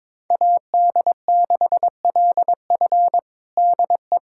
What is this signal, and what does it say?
Morse code, heard as a keyed tone, about 22 wpm, 700 Hz. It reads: AD6LF DE
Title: USA Callsign DE Callsign - 5 Characters - 22wpm